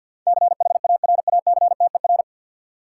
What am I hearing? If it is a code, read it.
Morse code, heard as a keyed tone, about 53 wpm, 700 Hz. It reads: CHARACTER
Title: Sentences from Top 1200 Words - New Word Drill - 50wpm